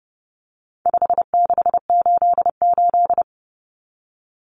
Morse code, heard as a keyed tone, about 29 wpm, 700 Hz. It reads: 5688
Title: Common Cut Numbers - 4-Digit 30wpm